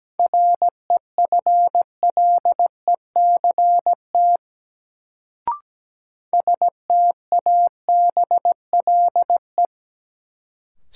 Morse code, heard as a keyed tone, about 17 wpm, 700 Hz. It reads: REFLECT STABLE
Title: Sentences from Top 1500 Words - New Word Drill - 17wpm